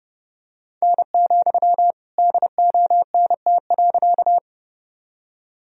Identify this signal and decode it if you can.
Morse code, heard as a keyed tone, about 32 wpm, 700 Hz. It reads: N, DONT.
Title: Sentences from Top 300 Words and Abbreviations 30wpm